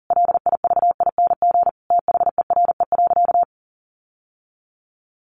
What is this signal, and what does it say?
Morse code, heard as a keyed tone, about 40 wpm, 700 Hz. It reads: LIVING THERE.
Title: Sentences from Top 1500 Words 40wpm